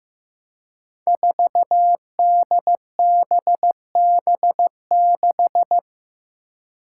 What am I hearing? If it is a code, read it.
Morse code, heard as a keyed tone, about 15 wpm, 700 Hz. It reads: 4DBB6
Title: Five Challenging Letter-Number Combinations 15wpm